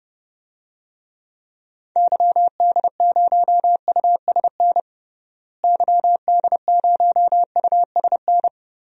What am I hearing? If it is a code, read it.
Morse code, heard as a keyed tone, about 31 wpm, 700 Hz. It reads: YD0USN YD0USN
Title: DXCC-100 Call Signs - 6 Characters - Repeated 3x - 30wpm